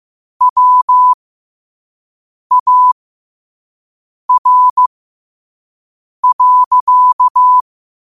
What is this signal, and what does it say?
Morse code, heard as a keyed tone, about 15 wpm, spaced as for 5 wpm, 1000 Hz. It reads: WAR.
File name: qotdfiles / 5wpm